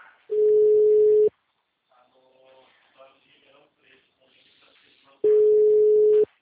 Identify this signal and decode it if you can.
Ringback tone (Europe, 425 Hz), with speech.